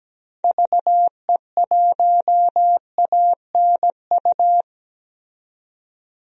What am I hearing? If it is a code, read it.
Morse code, heard as a keyed tone, about 17 wpm, 700 Hz. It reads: VE1ANU
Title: DXCC-100 Call Signs - 6 Characters - 17wpm